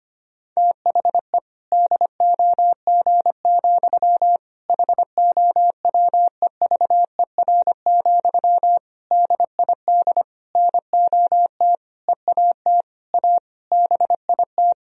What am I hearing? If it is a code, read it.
Morse code, heard as a keyed tone, about 25 wpm, 700 Hz. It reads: THE DOG, HOWEVER, DID NOT EAT A BIT
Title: Sentences from Top 1000 Words - Review Part 3 - 25wpm